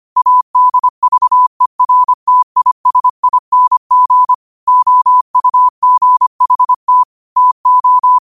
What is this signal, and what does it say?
Morse code, heard as a keyed tone, about 25 wpm, 1000 Hz. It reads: ADVERTISING OUGHT TO
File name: qotdfiles / 25wpm